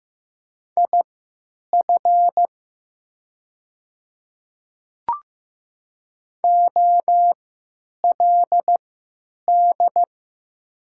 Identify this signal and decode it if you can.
Morse code, heard as a keyed tone, about 15 wpm, spaced as for 9 wpm, 700 Hz. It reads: IF OLD
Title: 3x Character Spacing - Top 200 Words - 15wpm